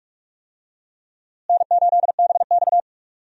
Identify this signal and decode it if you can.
Morse code, heard as a keyed tone, about 46 wpm, 700 Hz. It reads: N8BX
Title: USA Call Signs - 4 Characters - Repeated 3x - 45wpm